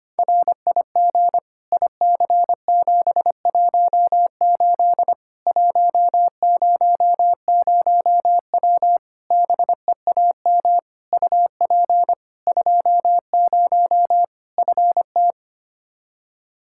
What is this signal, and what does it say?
Morse code, heard as a keyed tone, about 25 wpm, 700 Hz. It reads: RIG IC718 100W BEAM UP 20 FT
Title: K1USN SST QSO Practice 25wpm